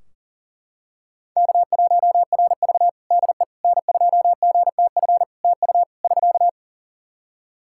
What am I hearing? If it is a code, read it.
Morse code, heard as a keyed tone, about 40 wpm, 700 Hz. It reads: K1RV DE N2GTF TU <SK>
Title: Practice QSO - Exchange 4 - Ending - 40wpm